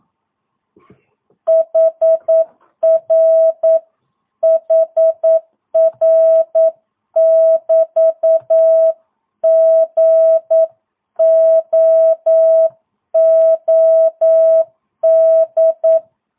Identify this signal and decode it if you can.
Morse code, heard as a keyed tone, about 9 wpm, 650 Hz. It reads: HR HR <BT> GOOD
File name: jh8cbh / 2022cw